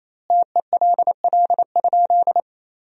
Morse code, heard as a keyed tone, about 28 wpm, 700 Hz. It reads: TELL?
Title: 2x Word-Spacing - Sentences from Top 200 Words - 28wpm